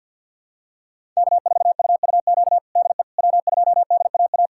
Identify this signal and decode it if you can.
Morse code, heard as a keyed tone, about 47 wpm, 700 Hz. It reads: K4UUQ DE W2DAA
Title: USA Callsign DE Callsign - 5 Characters - 50wpm